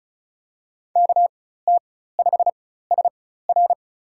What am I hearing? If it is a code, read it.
Morse code, heard as a keyed tone, about 34 wpm, spaced as for 16 wpm, 700 Hz. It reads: KT5SR
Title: USA Call Signs - 5 Characters - 4x Character Spacing - 35wpm